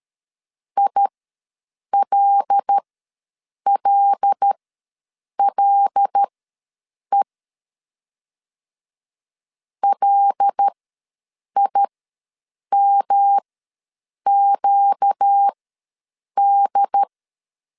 Morse code, heard as a keyed tone, about 13 wpm, spaced as for 7 wpm, 800 Hz. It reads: ILLLE LIMQD
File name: LEZIONI / 5 LBVC / 5 L ws 12wpm 2958